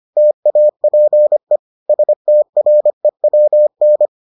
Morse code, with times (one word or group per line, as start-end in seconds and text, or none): 0.2-1.6 TAPE
1.9-4.1 STREWN